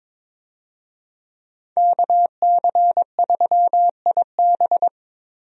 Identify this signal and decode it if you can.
Morse code, heard as a keyed tone, about 22 wpm, 700 Hz. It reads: KC3IB
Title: USA Call Signs - 5 Characters - Repeated 3x - 22wpm